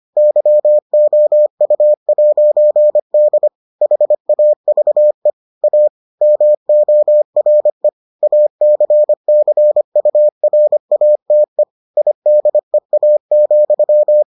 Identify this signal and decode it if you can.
Morse code, heard as a keyed tone, about 25 wpm, 600 Hz. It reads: YOU'D HAVE A MORE ACCURATE IDEA,